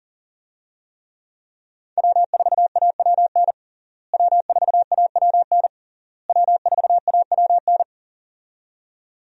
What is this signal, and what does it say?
Morse code, heard as a keyed tone, about 40 wpm, 700 Hz. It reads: W4AWN W4AWN W4AWN
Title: USA Call Signs - 5 Characters - Repeated 3x - 40wpm